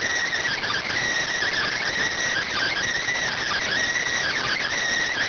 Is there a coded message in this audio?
Yes: Emergency Alert System (SAME) tones from 0.5 to 4.7 s, heard as harsh buzzing bursts.